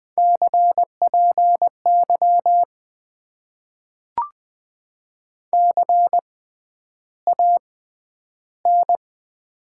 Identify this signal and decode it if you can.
Morse code, heard as a keyed tone, about 20 wpm, spaced as for 10 wpm, 700 Hz. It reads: CPY CAN
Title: Top 90 QSO Elements - 6x Character Spacing - 20wpm